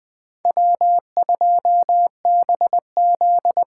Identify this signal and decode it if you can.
Morse code, heard as a keyed tone, about 21 wpm, 700 Hz. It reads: W2BZ